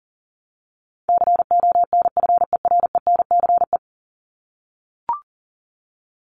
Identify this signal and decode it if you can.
Morse code, heard as a keyed tone, about 40 wpm, 700 Hz. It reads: CONFERENCE
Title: Top 1751-2000 Words 40wpm